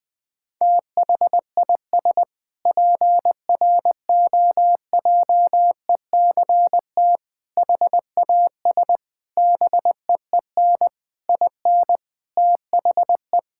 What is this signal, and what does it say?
Morse code, heard as a keyed tone, about 20 wpm, 700 Hz. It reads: THIS PROJECT HAS BEEN IN THE
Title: Sentences with 6-Word Phrases - 20wpm